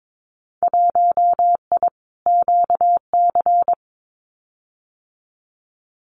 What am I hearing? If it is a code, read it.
Morse code, heard as a keyed tone, about 22 wpm, 700 Hz. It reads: 1I QC
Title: Winter Field Day Exchanges 22wpm